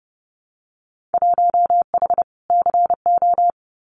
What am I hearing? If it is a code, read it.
Morse code, heard as a keyed tone, about 30 wpm, 700 Hz. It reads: 1H CO
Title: Winter Field Day Exchanges 30wpm